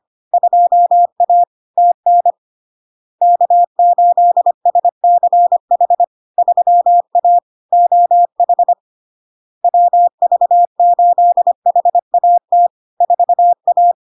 Morse code, heard as a keyed tone, about 25 wpm, 700 Hz. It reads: 2A TN K8SCH 3A OH WV8HAT 4A
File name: LARK-FD-2022-Log-in-CW-25wpm-700-Hz